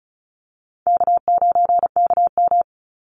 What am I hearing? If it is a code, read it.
Morse code, heard as a keyed tone, about 34 wpm, 700 Hz. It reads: K9KM
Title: Easy Call Signs Repeated 2x - 35wpm